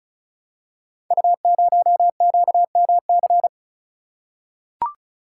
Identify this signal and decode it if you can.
Morse code, heard as a keyed tone, about 36 wpm, 690 Hz. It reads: U0QMC
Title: Code Groups - 5 Letter:Numbers - 35wpm